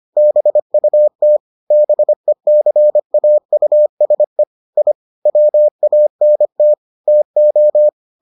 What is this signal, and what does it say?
Morse code, heard as a keyed tone, about 25 wpm, 600 Hz. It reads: BUT BECAUSE I WANT TO